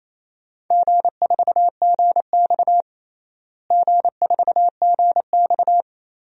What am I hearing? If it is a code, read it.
Morse code, heard as a keyed tone, about 28 wpm, 700 Hz. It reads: G4GX G4GX
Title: DXCC-100 Call Signs - 4 Characters - Repeated 3x - 28wpm